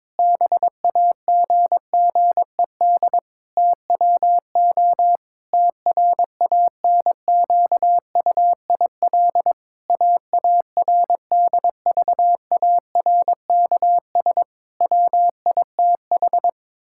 Morse code, heard as a keyed tone, about 21 wpm, 700 Hz. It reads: BAGGED TWO TRANQUIL AARDVARKS WITH
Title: Pangram Challenge v2 - 22wpm